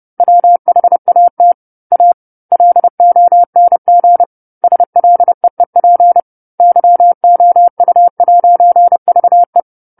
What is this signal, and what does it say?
Morse code, heard as a keyed tone, about 30 wpm, 700 Hz. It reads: WHAT A LONG SLEEP YOU'VE
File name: 10-Chapter-10 30 WPM